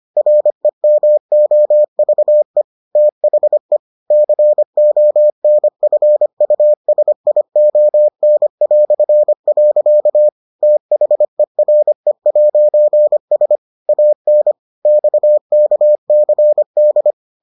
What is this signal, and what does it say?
Morse code, heard as a keyed tone, about 25 wpm, 600 Hz. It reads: REMOVE THE CONFUSION". THERE'S AN XKCD